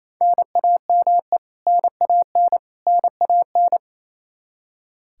Morse code, heard as a keyed tone, about 27 wpm, 700 Hz. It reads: NAME NAN NAN
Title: K1USN SST QSO Practice 28wpm